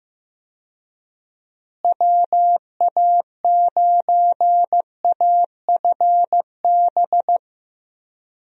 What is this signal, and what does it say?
Morse code, heard as a keyed tone, about 15 wpm, 700 Hz. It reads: WA9AFB
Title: USA Call Signs - 6 Characters - Repeated 3x - 15wpm